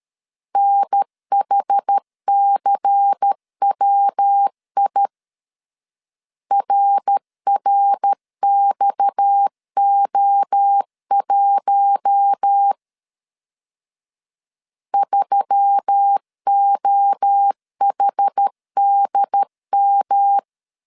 Morse code, heard as a keyed tone, about 13 wpm, 800 Hz. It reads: NHCWI RRXO1 3OHDM
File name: CW / DET / 1 6 1500C S A 12wpm 3440